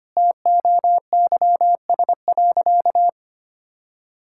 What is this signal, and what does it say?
Morse code, heard as a keyed tone, about 25 wpm, 700 Hz. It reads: TOYS.
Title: Pangram Challenge v2 - 25wpm